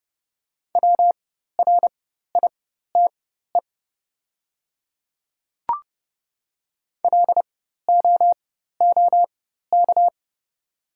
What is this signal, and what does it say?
Morse code, heard as a keyed tone, about 30 wpm, spaced as for 14 wpm, 690 Hz. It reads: WRITE LOOK
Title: 4x Character Spacing - Top 100 Words - 30wpm